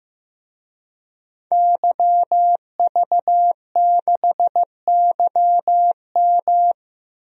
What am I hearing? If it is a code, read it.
Morse code, heard as a keyed tone, about 15 wpm, 700 Hz. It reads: YV6YM
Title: DXCC-100 Call Signs - 5 Characters - 15wpm